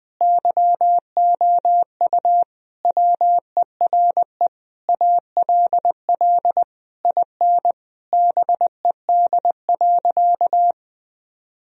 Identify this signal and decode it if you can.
Morse code, heard as a keyed tone, about 21 wpm, 700 Hz. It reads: YOU WERE ALL IN BED.
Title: Sentences from Top 1000 Words - Review Part 1 - 20wpm